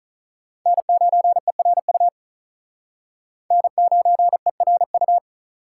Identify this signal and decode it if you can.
Morse code, heard as a keyed tone, about 44 wpm, 700 Hz. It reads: N9ERU N9ERU